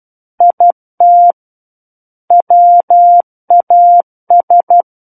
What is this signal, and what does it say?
Morse code, heard as a keyed tone, about 12 wpm, 700 Hz. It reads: IT WAS
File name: Book One / 03-Chapter 3 - 12 WPM